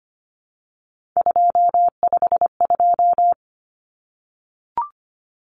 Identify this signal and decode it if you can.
Morse code, heard as a keyed tone, about 25 wpm, 700 Hz. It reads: 252
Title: Common Cut Numbers - 3-Digit 25wpm